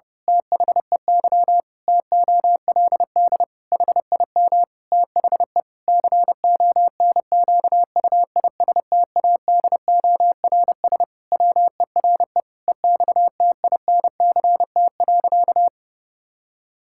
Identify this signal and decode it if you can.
Morse code, heard as a keyed tone, about 32 wpm, 700 Hz. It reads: THEY TOLD HIM THE CONQUISTADORS WERE EXTINCT.
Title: Pangram Challenge v2 - 30wpm